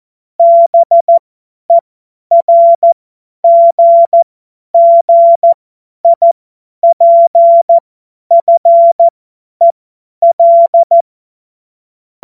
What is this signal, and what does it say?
Morse code, heard as a keyed tone, about 14 wpm, spaced as for 10 wpm, 680 Hz. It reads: BERGGIPFEL